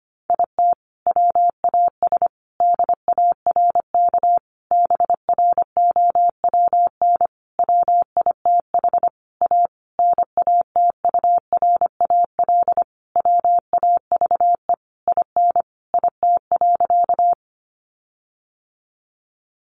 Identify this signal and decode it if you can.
Morse code, heard as a keyed tone, about 25 wpm, 700 Hz. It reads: IT WAS DARK BROWN WITH A NATURAL WAVE IN IT.
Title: Sentences from Top 1400 Words 25wpm